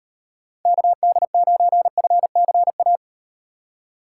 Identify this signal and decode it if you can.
Morse code, heard as a keyed tone, about 38 wpm, 700 Hz. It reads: KD9FCA